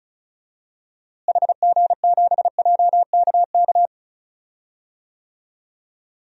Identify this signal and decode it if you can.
Morse code, heard as a keyed tone, about 36 wpm, 700 Hz. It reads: HG7JKK
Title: DXCC-100 Call Signs - 6 Characters - Repeated 3x - 35wpm